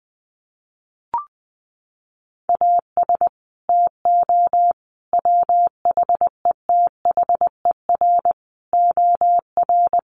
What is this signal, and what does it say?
Morse code, heard as a keyed tone, about 20 wpm, 700 Hz. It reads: AS TO WHETHER OR
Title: Sets of 5 Words Encore 20wpm